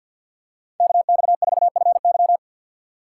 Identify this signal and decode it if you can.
Morse code, heard as a keyed tone, about 52 wpm, 700 Hz. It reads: KX4FY
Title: USA Call Signs - 5 Characters - 4x Character Spacing - 50wpm